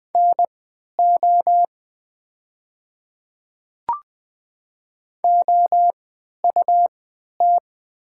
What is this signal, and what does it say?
Morse code, heard as a keyed tone, about 20 wpm, spaced as for 11 wpm, 700 Hz. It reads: NO OUT
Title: 3x Character Spacing - Top 100 Words - 20wpm